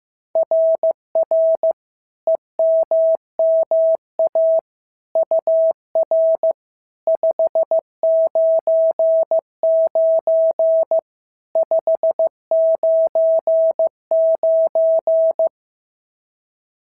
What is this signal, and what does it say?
Morse code, heard as a keyed tone, about 15 wpm, 650 Hz. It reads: RR EMMA UR 599 599
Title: SOTA QSO Practice 15wpm